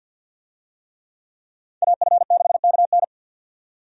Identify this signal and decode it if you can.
Morse code, heard as a keyed tone, about 47 wpm, 700 Hz. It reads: AF6KN
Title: USA Call Signs - 5 Characters - 50wpm